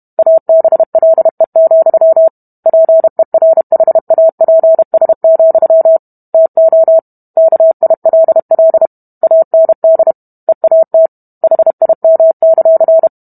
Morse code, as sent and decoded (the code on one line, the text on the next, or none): .- -... .-.. --..-- / .--. .-. .... .- .--. ... --..-- / - --- / -.- .. .-.. .-.. / .- -. -.. / . .- - / .... .. -- -.-.-.
ABLE, PERHAPS, TO KILL AND EAT HIM;